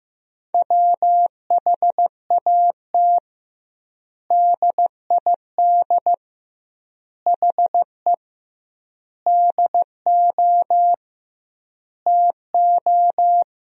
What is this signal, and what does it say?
Morse code, heard as a keyed tone, about 16 wpm, 700 Hz. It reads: WHAT DID HE DO TO
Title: 2x Word-Spacing - Sentences from Top 100 Words - 15wpm